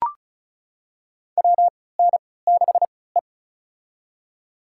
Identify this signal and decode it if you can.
Morse code, heard as a keyed tone, about 36 wpm, spaced as for 20 wpm, 700 Hz. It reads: WN6E
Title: USA Call Signs - 4 Characters - 3x Character Spacing - 35wpm